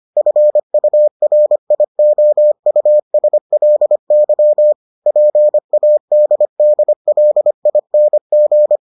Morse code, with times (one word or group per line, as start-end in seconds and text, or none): 0.2-4.7 FURIOUSLY
5.1-8.8 PADDLING